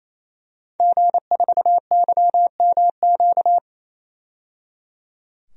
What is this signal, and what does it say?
Morse code, heard as a keyed tone, about 28 wpm, 700 Hz. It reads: G4YMQ
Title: DXCC-100 Call Signs - 5 Characters - 28wpm